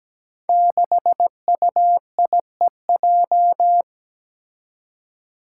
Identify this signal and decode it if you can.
Morse code, heard as a keyed tone, about 17 wpm, 700 Hz. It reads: 6UIEJ
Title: Five Challenging Letter-Number Combinations 17wpm